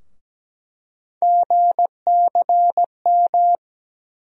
Morse code, heard as a keyed tone, about 17 wpm, 700 Hz. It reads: GCM
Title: USA Call Sign Postfixes - 3 Letters 17wpm